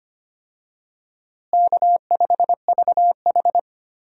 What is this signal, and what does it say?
Morse code, heard as a keyed tone, about 25 wpm, 700 Hz. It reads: K5VH
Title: USA Call Signs - 4 Characters - Repeated 3x - 25wpm